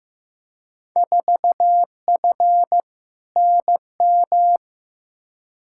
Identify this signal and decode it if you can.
Morse code, heard as a keyed tone, about 15 wpm, 700 Hz. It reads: 4F NM
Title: Field Day Exchanges Encore - 15wpm